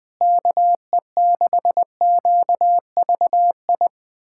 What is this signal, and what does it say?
Morse code, heard as a keyed tone, about 20 wpm, 700 Hz. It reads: KE6QVI
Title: USA Callsign DE Callsign - 6 Characters - 20wpm